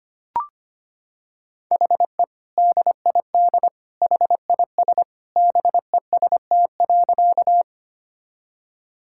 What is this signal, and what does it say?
Morse code, heard as a keyed tone, about 25 wpm, 700 Hz. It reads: HE DID HIS BEST.
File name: practice / 25wpm / Sentences with 2-Word Phrases - 25wpm